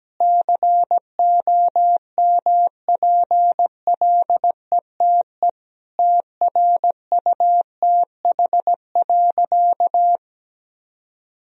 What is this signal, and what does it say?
Morse code, heard as a keyed tone, about 17 wpm, 700 Hz. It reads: COMPLETE TRUTH.
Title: Sentences from Top 1000 Words - Review Part 2 - 17wpm